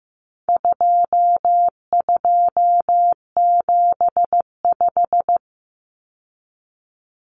Morse code, heard as a keyed tone, about 15 wpm, 700 Hz. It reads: 2275
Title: Common Cut Numbers - 4-Digit 15wpm